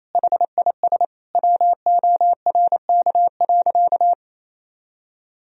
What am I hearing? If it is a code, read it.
Morse code, heard as a keyed tone, about 28 wpm, 700 Hz. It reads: HIS WORK.
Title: Sentences from Top 1000 Words - Review Part 4 - 28wpm